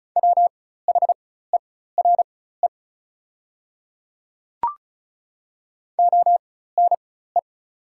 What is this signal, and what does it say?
Morse code, heard as a keyed tone, about 36 wpm, spaced as for 16 wpm, 700 Hz. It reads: WHERE ONE
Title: 4x Character Spacing - Top 200 Words - 35wpm